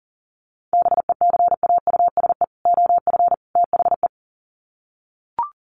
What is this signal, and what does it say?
Morse code, heard as a keyed tone, about 38 wpm, 700 Hz. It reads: BECAUSE OF THE
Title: Sets of 3 Words 40wpm